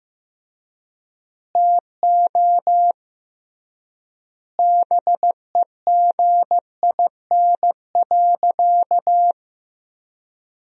Morse code, heard as a keyed tone, about 15 wpm, 700 Hz. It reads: TO BEGIN.
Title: 3x Word-Spacing - Sentences from Top 300 Words - 15wpm